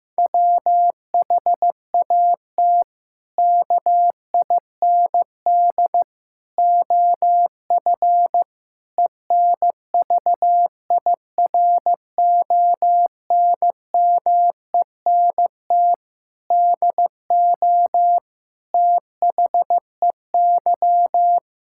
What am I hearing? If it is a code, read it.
Morse code, heard as a keyed tone, about 15 wpm, 700 Hz. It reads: WHAT KIND OF ENVIRONMENT DO THEY
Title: Sentences from Top 1000 Words - Review Part 5 - 15wpm